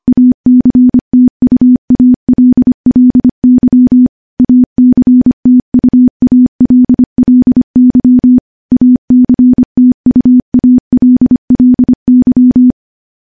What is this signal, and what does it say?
Morse code, heard as a keyed tone, about 25 wpm, 250 Hz. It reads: ACTUALLY ACTUALLY ACTUALLY
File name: morse / morse-training-25WPM